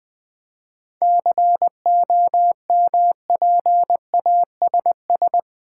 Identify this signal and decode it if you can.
Morse code, heard as a keyed tone, about 20 wpm, 700 Hz. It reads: COMPASS
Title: Easy Words with Hints - 20wpm